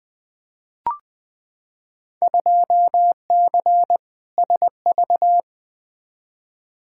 Morse code, heard as a keyed tone, about 20 wpm, 700 Hz. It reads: 2C SV
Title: Field Day Exchanges Encore - 20wpm